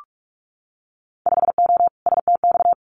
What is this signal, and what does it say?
Morse code, heard as a keyed tone, about 44 wpm, 700 Hz. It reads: 5O STX
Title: Winter Field Day Exchanges 45wpm